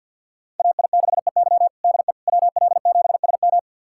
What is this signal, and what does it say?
Morse code, heard as a keyed tone, about 50 wpm, 700 Hz. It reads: AI6EY DE WL7SM
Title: USA Callsign DE Callsign - 5 Characters - 50wpm